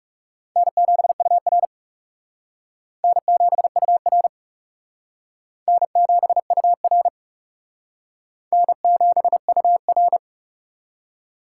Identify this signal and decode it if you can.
Morse code, heard as a keyed tone, about 37 wpm, 700 Hz. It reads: N7UR N7UR N7UR N7UR
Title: Speed Racing - USA Call Signs 30wpm